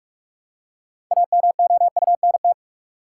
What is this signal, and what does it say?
Morse code, heard as a keyed tone, about 47 wpm, 700 Hz. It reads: AMOUNT
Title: Six Letter Words - 45wpm